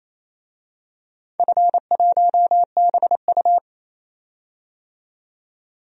Morse code, heard as a keyed tone, about 27 wpm, 700 Hz. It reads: F1BU
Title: DXCC-100 Call Signs - 4 Characters - 28wpm